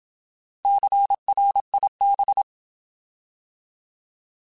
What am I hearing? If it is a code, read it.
Morse code, heard as a keyed tone, about 27 wpm, 800 Hz. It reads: CRIB
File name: Alphabetical / Alphabetical mp3s / Cs @25